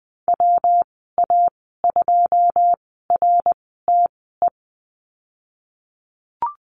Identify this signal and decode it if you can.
Morse code, heard as a keyed tone, about 20 wpm, spaced as for 15 wpm, 700 Hz. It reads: WA2RTE
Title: 140 Lesson 23g - Letter f - Call Signs 20wpm